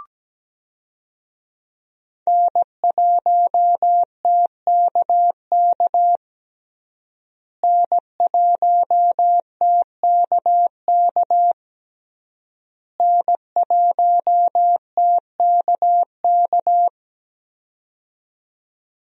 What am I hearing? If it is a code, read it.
Morse code, heard as a keyed tone, about 17 wpm, 700 Hz. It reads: N1TKK N1TKK N1TKK
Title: USA Call Signs - 5 Characters - Repeated 3x - 17wpm